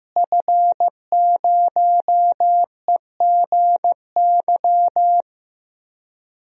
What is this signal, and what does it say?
Morse code, heard as a keyed tone, about 15 wpm, 700 Hz. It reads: F0EGY
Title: DXCC-100 Call Signs - 5 Characters - Repeated 2x - 15wpm